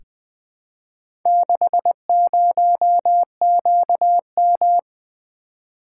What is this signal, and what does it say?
Morse code, heard as a keyed tone, about 20 wpm, 700 Hz. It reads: 60QM
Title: Four Letter-Number Combinations - 20wpm